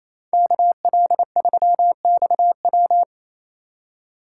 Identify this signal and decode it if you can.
Morse code, heard as a keyed tone, about 37 wpm, 700 Hz. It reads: KL3XW